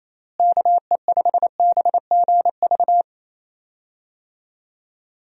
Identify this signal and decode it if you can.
Morse code, heard as a keyed tone, about 28 wpm, 700 Hz. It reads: KE5BGV
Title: USA Call Signs - 6 Characters - Repeated 3x - 28wpm